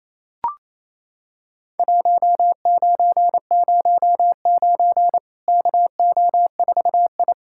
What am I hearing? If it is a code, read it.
Morse code, heard as a keyed tone, about 28 wpm, 700 Hz. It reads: 1909 KO4I
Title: Worked All Europe DX Contest - QTC Exchange - 28wpm